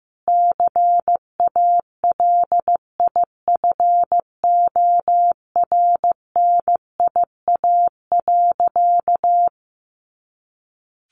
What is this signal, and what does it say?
Morse code, heard as a keyed tone, about 15 wpm, 700 Hz. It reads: CALIFORNIA.
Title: Easy Phrases 15wpm